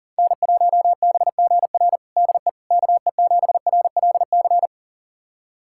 Morse code, heard as a keyed tone, about 40 wpm, 700 Hz. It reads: N1BGR DE KE7RLC
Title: USA Callsign DE Callsign 40wpm